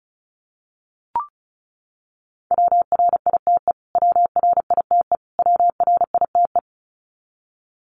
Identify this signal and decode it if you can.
Morse code, heard as a keyed tone, about 34 wpm, 700 Hz. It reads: WRITE WRITE WRITE
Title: Top 200 Words Repeated 35wpm